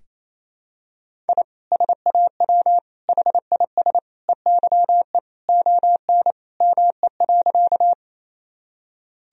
Morse code, heard as a keyed tone, about 28 wpm, 700 Hz. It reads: I SAW HIS EYE ON ME.
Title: Sentences from Top 1000 Words - Review Part 1 - 28wpm